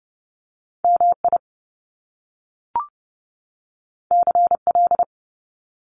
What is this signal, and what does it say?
Morse code, heard as a keyed tone, about 30 wpm, 700 Hz. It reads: MI CL
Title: Top Two Letter Combinations in Words 30wpm